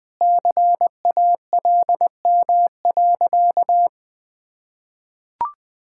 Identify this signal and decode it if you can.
Morse code, heard as a keyed tone, about 20 wpm, 700 Hz. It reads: CALM.
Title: Sentences with 2-Word Phrases - 20wpm